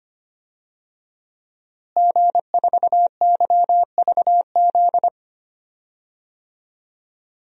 Morse code, heard as a keyed tone, about 25 wpm, 700 Hz. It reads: G4YVZ
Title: DXCC-100 Call Signs - 5 Characters - Repeated 2x - 25wpm